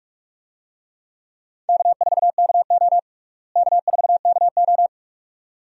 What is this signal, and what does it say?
Morse code, heard as a keyed tone, about 46 wpm, 700 Hz. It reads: K4KO K4KO
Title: USA Call Signs - 4 Characters - Repeated 2x - 45wpm